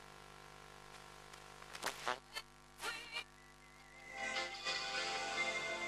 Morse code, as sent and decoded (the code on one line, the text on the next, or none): none